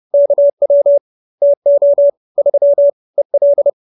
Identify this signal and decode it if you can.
Morse code, heard as a keyed tone, about 30 wpm, 570 Hz. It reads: KW TO 3 EL
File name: QSO309 30